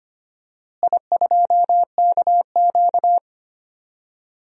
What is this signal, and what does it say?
Morse code, heard as a keyed tone, about 25 wpm, 700 Hz. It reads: I2KQ
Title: DXCC-100 Call Signs - 4 Characters - Repeated 3x - 25wpm